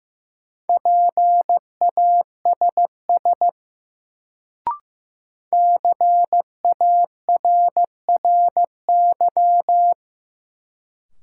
Morse code, heard as a keyed tone, about 15 wpm, 700 Hz. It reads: PASS CARRY
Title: Sentences from Top 400 Words - New Word Drill - 15wpm